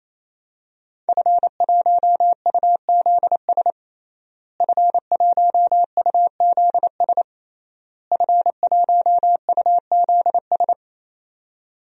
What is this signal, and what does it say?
Morse code, heard as a keyed tone, about 28 wpm, 700 Hz. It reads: F1UZS F1UZS F1UZS
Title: DXCC-100 Call Signs - 5 Characters - Repeated 3x - 28wpm